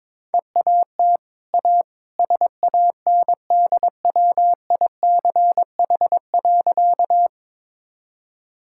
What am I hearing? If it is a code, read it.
Morse code, heard as a keyed tone, about 22 wpm, 700 Hz. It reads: EAT A SANDWICH.
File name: practice / 22wpm / Easy Sentences with Hints - 22wpm